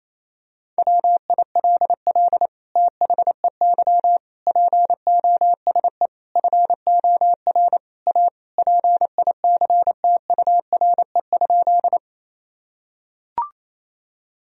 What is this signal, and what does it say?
Morse code, heard as a keyed tone, about 28 wpm, 700 Hz. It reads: WILL THEY POSE FOR A PICTURE?
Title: Sentences from Top 1000 Words - Review Part 2 - 28wpm